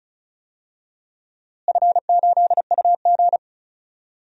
Morse code, heard as a keyed tone, about 36 wpm, 700 Hz. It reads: F8UG
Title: DXCC-100 Call Signs - 4 Characters - Repeated 2x - 35wpm